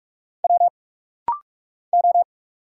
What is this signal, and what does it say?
Morse code, heard as a keyed tone, about 46 wpm, 700 Hz. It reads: W O
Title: Single Letters - Rapid-Fire 45wpm